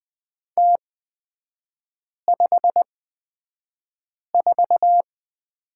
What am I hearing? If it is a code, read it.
Morse code, heard as a keyed tone, about 20 wpm, spaced as for 5 wpm, 690 Hz. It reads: T54
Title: 229 Lesson 37e - Letter x - Three Characters 20wpm